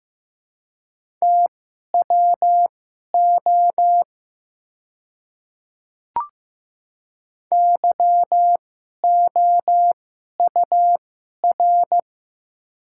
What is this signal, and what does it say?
Morse code, heard as a keyed tone, about 15 wpm, spaced as for 11 wpm, 700 Hz. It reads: TWO YOUR
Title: 2x Character Spacing - Top 100 Words - 15wpm